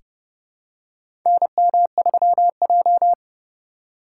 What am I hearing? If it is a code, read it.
Morse code, heard as a keyed tone, about 31 wpm, 700 Hz. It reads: NM3J
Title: USA Call Signs - 4 Characters - 3x Character Spacing - 30wpm